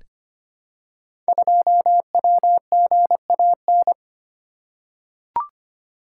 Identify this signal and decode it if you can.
Morse code, heard as a keyed tone, about 25 wpm, 700 Hz. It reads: 2WGAN
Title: Code Groups - 5 Letter:Numbers - 25wpm